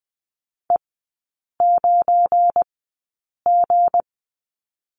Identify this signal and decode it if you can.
Morse code, heard as a keyed tone, about 20 wpm, spaced as for 8 wpm, 700 Hz. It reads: E9G
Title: 218 Lesson 35e - Number 0 - Three Characters 20wpm